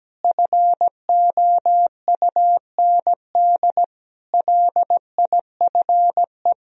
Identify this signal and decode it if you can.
Morse code, heard as a keyed tone, about 17 wpm, 700 Hz. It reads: FOUND LIFE
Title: Sentences from Top 1000 Words - Review Part 2 - 17wpm